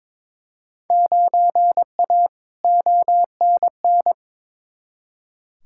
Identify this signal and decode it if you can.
Morse code, heard as a keyed tone, about 22 wpm, 700 Hz. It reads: 9A ONN
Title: Field Day Exchanges Encore - 22wpm